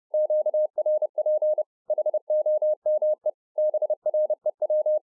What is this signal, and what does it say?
Morse code, heard as a keyed tone, about 30 wpm, 610 Hz. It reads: QRP HOME BREW